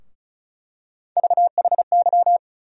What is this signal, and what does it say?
Morse code, heard as a keyed tone, about 34 wpm, 700 Hz. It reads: VHY
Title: USA Call Sign Postfixes - 3 Letters 35wpm